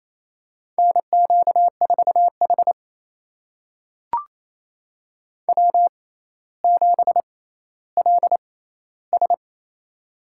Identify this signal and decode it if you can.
Morse code, heard as a keyed tone, about 28 wpm, spaced as for 14 wpm, 700 Hz. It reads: NQ4H W7LS
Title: USA Call Signs - 4 Characters - 6x Character Spacing - 28wpm